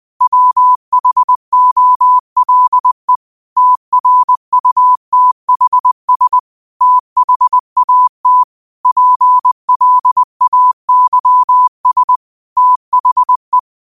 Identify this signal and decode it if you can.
Morse code, heard as a keyed tone, about 20 wpm, 1000 Hz. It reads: WHOLE TRUTHS THAT PLAYS THE
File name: qotdfiles / 20wpm